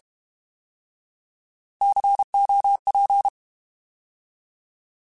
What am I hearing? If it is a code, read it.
Morse code, heard as a keyed tone, about 32 wpm, 800 Hz. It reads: COP
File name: Alphabetical / Alphabetical mp3s / Cs @30